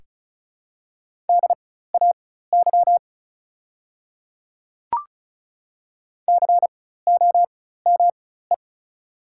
Morse code, heard as a keyed tone, about 36 wpm, spaced as for 16 wpm, 690 Hz. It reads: DAY COME
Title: 4x Character Spacing - Top 100 Words - 35wpm